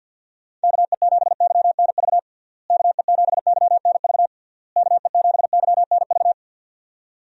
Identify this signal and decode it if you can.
Morse code, heard as a keyed tone, about 48 wpm, 700 Hz. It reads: KE7YNV KE7YNV KE7YNV
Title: USA Call Signs - 6 Characters - Repeated 3x - 50wpm